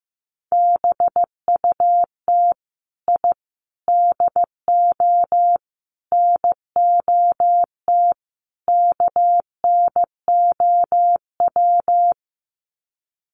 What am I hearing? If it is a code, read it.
Morse code, heard as a keyed tone, about 15 wpm, 700 Hz. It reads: BUT I DO NOT KNOW
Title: Sets of 5 Words 15wpm